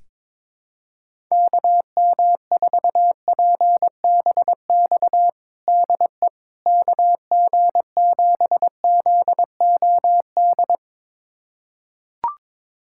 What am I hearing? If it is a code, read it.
Morse code, heard as a keyed tone, about 22 wpm, 700 Hz. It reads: KM4PBX DE KG7ZOD
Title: USA Callsign DE Callsign - 6 Characters - 22wpm